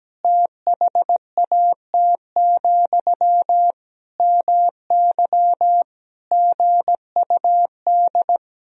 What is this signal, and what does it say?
Morse code, heard as a keyed tone, about 17 wpm, 700 Hz. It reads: THAT, MY GUD
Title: Sentences from Top 400 Words and Abbreviations 17wpm